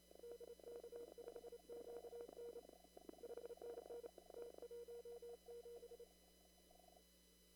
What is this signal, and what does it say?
Morse code, heard as a keyed tone, about 29 wpm, 470 Hz, with noise. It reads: DK5QN 5NN T17
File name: CW / kt5j wpx cw-op-k5pi